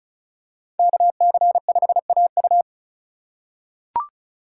Morse code, heard as a keyed tone, about 36 wpm, 690 Hz. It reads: KC5AU
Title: USA Call Signs - 5 Characters - 3x Character Spacing - 35wpm